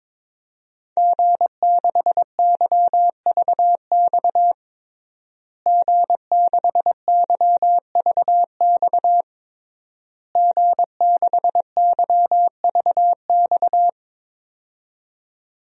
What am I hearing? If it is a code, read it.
Morse code, heard as a keyed tone, about 22 wpm, 700 Hz. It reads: G6YVX G6YVX G6YVX
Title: DXCC-100 Call Signs - 5 Characters - Repeated 3x - 22wpm